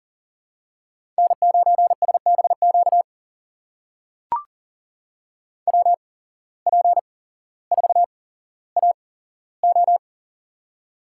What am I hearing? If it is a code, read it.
Morse code, heard as a keyed tone, about 37 wpm, spaced as for 17 wpm, 700 Hz. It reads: N9SBQ WP4AO
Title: USA Call Signs - 5 Characters - 8x Character Spacing - 40wpm